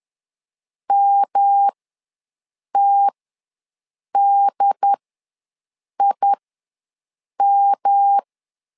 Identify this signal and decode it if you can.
Morse code, heard as a keyed tone, about 10 wpm, spaced as for 6 wpm, 800 Hz. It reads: MTDIM
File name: LEZIONI / 3 MWRO / 3 M ws 10wpm 2121